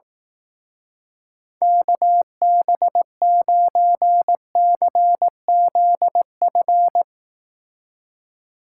Morse code, heard as a keyed tone, about 18 wpm, 700 Hz. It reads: KB9CZF
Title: Speed Racing - USA 2x3 Call Signs 15wpm